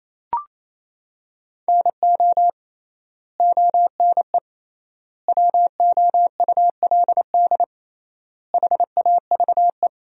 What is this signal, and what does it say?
Morse code, heard as a keyed tone, about 28 wpm, 700 Hz. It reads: NO ONE WOULD HAVE
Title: 3x Word-Spacing - Sentences from Top 300 Words - 28wpm